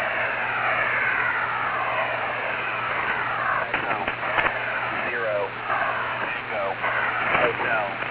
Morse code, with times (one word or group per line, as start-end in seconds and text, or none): none